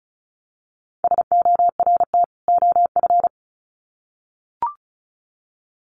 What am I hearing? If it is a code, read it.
Morse code, heard as a keyed tone, about 34 wpm, 700 Hz. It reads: SORT OF
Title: Sets of 2 Words 35wpm